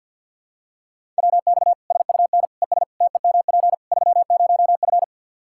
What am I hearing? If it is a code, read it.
Morse code, heard as a keyed tone, about 52 wpm, 700 Hz. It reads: WX SUN ES TEMP 30F
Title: Practice QSO - Exchange 2 - Info - 50wpm